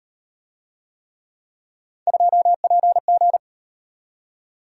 Morse code, heard as a keyed tone, about 35 wpm, 700 Hz. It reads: 2PG